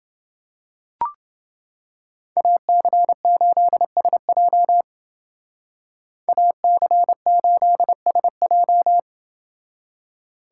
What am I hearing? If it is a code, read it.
Morse code, heard as a keyed tone, about 28 wpm, 700 Hz. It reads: AC8SJ AC8SJ